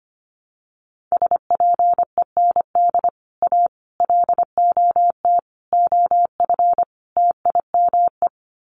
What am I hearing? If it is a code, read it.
Morse code, heard as a keyed tone, about 25 wpm, 700 Hz. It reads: SPEND A LOT OF TIME
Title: Sets of 5 Words 25wpm